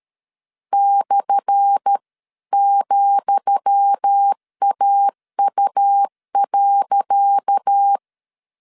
Morse code, heard as a keyed tone, about 13 wpm, 800 Hz. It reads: /,AU.
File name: LEZIONI / 10 SIGNS / 10D COMA 12wpm 2544